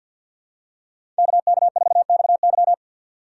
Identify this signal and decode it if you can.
Morse code, heard as a keyed tone, about 51 wpm, 700 Hz. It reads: KK4XY
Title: USA Call Signs - 5 Characters - 4x Character Spacing - 50wpm